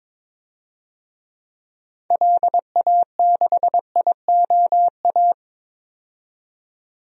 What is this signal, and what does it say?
Morse code, heard as a keyed tone, about 22 wpm, 700 Hz. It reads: LA6IOA
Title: DXCC-100 Call Signs - 6 Characters - 22wpm